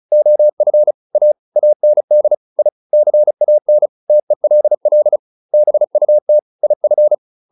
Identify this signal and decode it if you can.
Morse code, heard as a keyed tone, about 34 wpm, 600 Hz. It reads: OF A AND I CAN TELL BUT IF